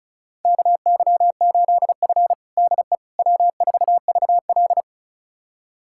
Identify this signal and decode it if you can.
Morse code, heard as a keyed tone, about 36 wpm, 690 Hz. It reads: KY8F DE W4VL
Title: USA Callsign DE Callsign - 4 Characters - 35wpm